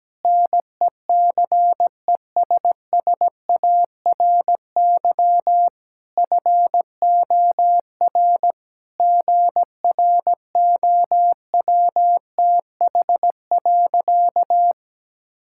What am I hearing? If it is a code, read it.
Morse code, heard as a keyed tone, about 17 wpm, 700 Hz. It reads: NECESSARY FOR GROWTH.
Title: Sentences from Top 1000 Words - Review Part 5 - 17wpm